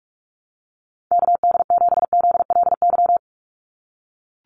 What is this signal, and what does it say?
Morse code, heard as a keyed tone, about 44 wpm, 700 Hz. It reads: KD7ZLY
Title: USA Call Signs 45wpm